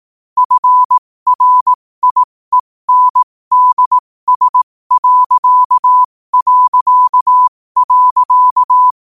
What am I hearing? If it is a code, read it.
Morse code, heard as a keyed tone, about 18 wpm, spaced as for 15 wpm, 1000 Hz. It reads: FRIENDS...
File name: qotdfiles / 15wpm